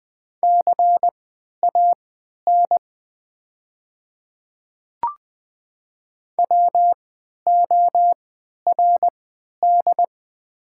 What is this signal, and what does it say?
Morse code, heard as a keyed tone, about 20 wpm, spaced as for 11 wpm, 700 Hz. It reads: CAN WORD
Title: 3x Character Spacing - Top 100 Words - 20wpm